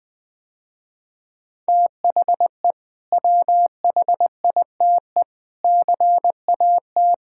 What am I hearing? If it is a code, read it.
Morse code, heard as a keyed tone, about 20 wpm, 700 Hz. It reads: THE WHITE CAT
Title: Sentences with 2-Word Phrases - Spoken Beginning - 20wpm